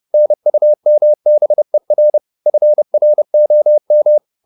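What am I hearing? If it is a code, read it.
Morse code, heard as a keyed tone, about 30 wpm, 600 Hz. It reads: NUMBER FROM
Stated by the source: Morse code at 30 wpm, the novel Never by Ken Follett in English, from CW med Gnister.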